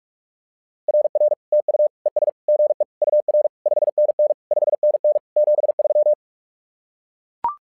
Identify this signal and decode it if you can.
Morse code, heard as a keyed tone, about 44 wpm, 600 Hz. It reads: RR TU ES GE UR 5NN 5NN 73
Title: SOTA QSO Practice 45wpm